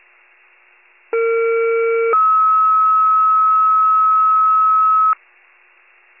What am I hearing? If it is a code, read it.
Two-tone page, heard as a one-second tone, then a three-second tone, about 460 then 1290 Hz. It is heard from 1.1 to 5.1 s.